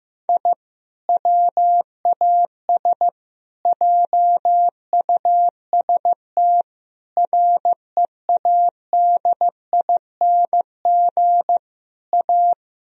Morse code, heard as a keyed tone, about 15 wpm, 700 Hz. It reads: I WAS JUST READING A
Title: Sentences from Top 1000 Words - Review Part 3 - 15wpm